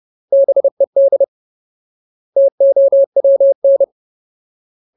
Morse code, heard as a keyed tone, about 30 wpm, 550 Hz. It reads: BED TOWN
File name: Ultimate Word List / 30WPM Ultimate